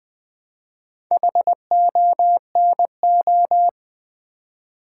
Morse code, heard as a keyed tone, about 20 wpm, 700 Hz. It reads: HONO
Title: Build A Word - 2 Letters At A Time - 6-Letter Words - 20wpm